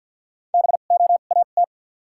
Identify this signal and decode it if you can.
Morse code, heard as a keyed tone, about 49 wpm, spaced as for 36 wpm, 700 Hz. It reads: BOAT
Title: 2x Character Spacing - Top 500 Words - 50wpm